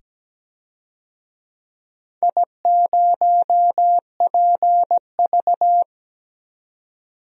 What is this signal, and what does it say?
Morse code, heard as a keyed tone, about 17 wpm, 700 Hz. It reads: I0PV
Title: DXCC-100 Call Signs - 4 Characters - 17wpm